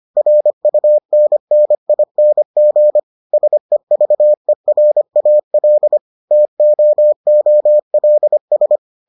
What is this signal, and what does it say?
Morse code, heard as a keyed tone, about 25 wpm, 600 Hz. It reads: RUNNING SEVERAL TOOLS